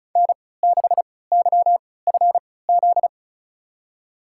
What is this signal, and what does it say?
Morse code, heard as a keyed tone, about 36 wpm, spaced as for 20 wpm, 700 Hz. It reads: N6YFZ
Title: USA Call Signs - 3x Character Spacing - 35wpm